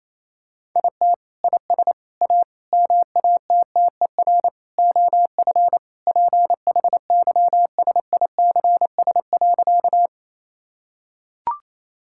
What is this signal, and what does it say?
Morse code, heard as a keyed tone, about 28 wpm, 700 Hz. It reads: IT IS A MATTER OF PHYSICS.
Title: Sentences from Top 1000 Words - Review Part 5 - 28wpm